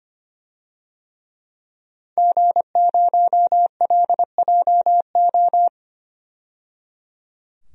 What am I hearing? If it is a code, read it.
Morse code, heard as a keyed tone, about 25 wpm, 700 Hz. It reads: G0LJO
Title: DXCC-100 Call Signs - 5 Characters - 25wpm